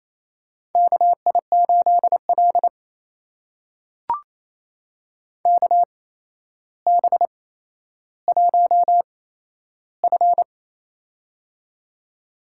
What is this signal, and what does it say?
Morse code, heard as a keyed tone, about 28 wpm, spaced as for 12 wpm, 700 Hz. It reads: KI8L KB1F
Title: USA Call Signs - 4 Characters - 8x Character Spacing - 28wpm